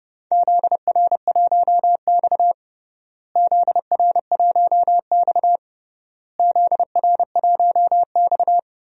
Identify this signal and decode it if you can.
Morse code, heard as a keyed tone, about 29 wpm, 700 Hz. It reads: ZR1X ZR1X ZR1X
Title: DXCC-100 Call Signs - 4 Characters - Repeated 3x - 30wpm